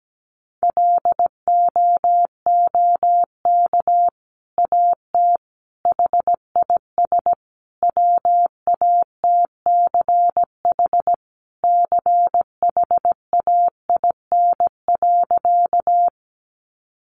Morse code, heard as a keyed tone, about 17 wpm, 700 Hz. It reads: LOOK AT HIS WATCH CHAIN.
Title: Sentences from Top 1500 Words 17wpm